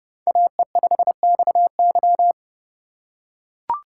Morse code, heard as a keyed tone, about 30 wpm, 700 Hz. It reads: AE5XY
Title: USA Call Signs - 6x Character Spacing - 30wpm